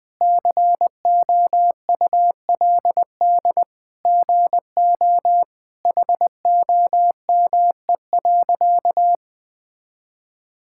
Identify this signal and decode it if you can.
Morse code, heard as a keyed tone, about 20 wpm, 700 Hz. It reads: COULD GO HOME.
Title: Sentences with 2-Word Phrases - 20wpm